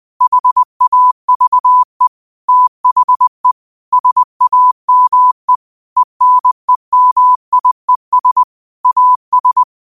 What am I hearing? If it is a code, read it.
Morse code, heard as a keyed tone, about 20 wpm, 1000 Hz. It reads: HAVE THE SAME ENEMIES AS
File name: qotdfiles / 20wpm